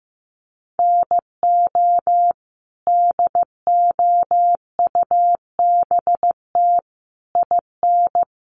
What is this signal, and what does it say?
Morse code, heard as a keyed tone, about 15 wpm, 700 Hz. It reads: NO DOUBT IN
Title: Sets of 5 Words Encore2 15wpm